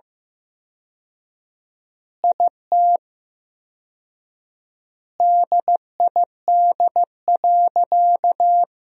Morse code, heard as a keyed tone, about 15 wpm, 700 Hz. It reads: IT DID.
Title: 4x Word-Spacing - Sentences from Top 500 Words - 15wpm